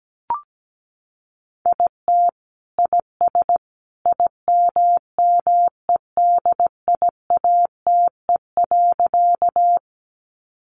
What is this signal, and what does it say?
Morse code, heard as a keyed tone, about 17 wpm, 700 Hz. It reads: IT IS IMMEDIATE.
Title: Sentences from Top 1200 Words 17wpm